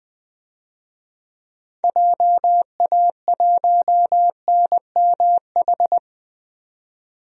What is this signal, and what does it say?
Morse code, heard as a keyed tone, about 20 wpm, 700 Hz. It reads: JA1NMH
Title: DXCC-100 Call Signs - 6 Characters - 20wpm